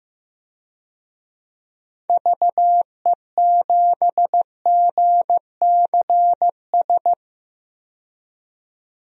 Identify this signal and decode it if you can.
Morse code, heard as a keyed tone, about 15 wpm, 700 Hz. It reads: VE7GCS
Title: DXCC-100 Call Signs - 6 Characters - 15wpm